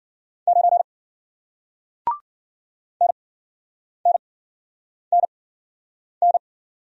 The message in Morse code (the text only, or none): 9 N N N N